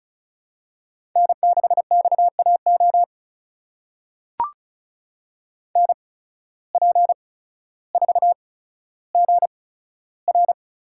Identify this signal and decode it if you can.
Morse code, heard as a keyed tone, about 36 wpm, spaced as for 15 wpm, 700 Hz. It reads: N6XAO NP4GR
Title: USA Call Signs - 5 Characters - 8x Character Spacing - 35wpm